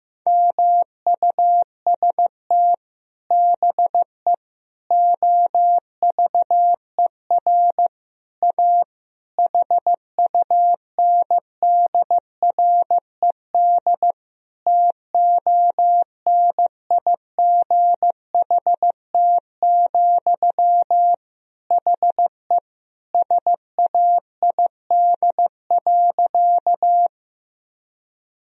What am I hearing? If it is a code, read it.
Morse code, heard as a keyed tone, about 15 wpm, 700 Hz. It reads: MUST BE OVER A HUNDRED TONIGHT, HE SAID.
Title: Sentences from Top 1000 Words - Review Part 2 - 15wpm